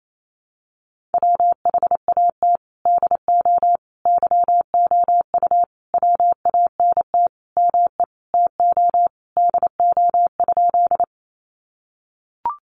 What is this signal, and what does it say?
Morse code, heard as a keyed tone, about 28 wpm, 700 Hz. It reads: WHAT DO YOU WANT ME TO DO?
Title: Sentences from Top 200 Words 28wpm